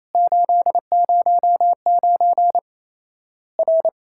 Morse code, heard as a keyed tone, about 28 wpm, 700 Hz. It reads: 809 R
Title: Worked All Europe DX Contest - QTC Exchange - 28wpm